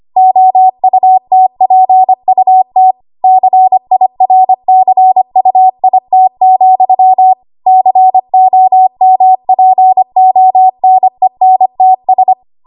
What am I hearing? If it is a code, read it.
Morse code, heard as a keyed tone, about 25 wpm, 750 Hz. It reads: OUTPUT CIRCUIT, COMPONENTS
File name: Archive / 25 WPM / 25WPM